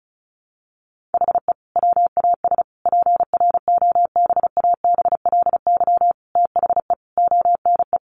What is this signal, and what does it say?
Morse code, heard as a keyed tone, about 34 wpm, 700 Hz. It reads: HE WAS PROBABLY THE ONE
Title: Sentences from Top 1300 Words 35wpm